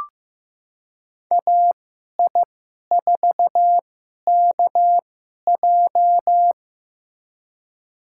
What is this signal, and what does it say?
Morse code, heard as a keyed tone, about 15 wpm, spaced as for 11 wpm, 700 Hz. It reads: AI4KJ
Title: USA Call Signs - 2x Character Spacing - 15wpm